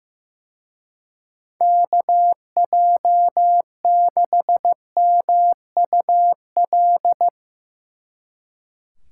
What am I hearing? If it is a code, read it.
Morse code, heard as a keyed tone, about 16 wpm, 700 Hz. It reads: KJ6MUL
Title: Speed Racing - USA 2x3 Call Signs 15wpm